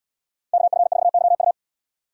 Morse code, heard as a keyed tone, about 73 wpm, 700 Hz. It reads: KF4YU